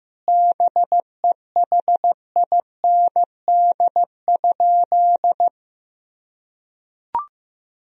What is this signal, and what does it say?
Morse code, heard as a keyed tone, about 15 wpm, 700 Hz. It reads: BEHIND?
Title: Sentences from Top 1000 Words - Review Part 2 - 15wpm